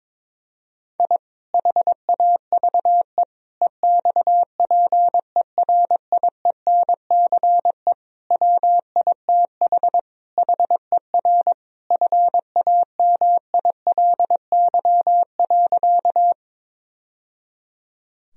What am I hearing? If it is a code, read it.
Morse code, heard as a keyed tone, about 22 wpm, 700 Hz. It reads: I HAVE EXPERIENCE WITH HER FAMILY.
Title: Sentences from Top 1000 Words - Review Part 3 - 22wpm